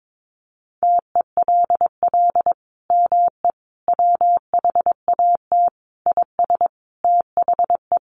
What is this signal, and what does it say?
Morse code, heard as a keyed tone, about 22 wpm, 700 Hz. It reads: TELL ME WHAT IS THE
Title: Sentences from Top 700 Words 22wpm